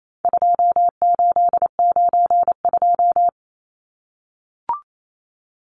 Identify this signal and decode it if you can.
Morse code, heard as a keyed tone, about 28 wpm, 700 Hz. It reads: 2892
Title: Numbers - 4-Digit 28wpm